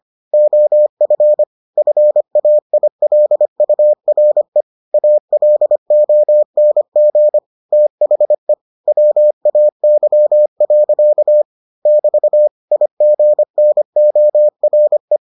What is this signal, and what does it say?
Morse code, heard as a keyed tone, about 25 wpm, 600 Hz. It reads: OF FAILURE ALONG THE WAY. <BT> IGNORE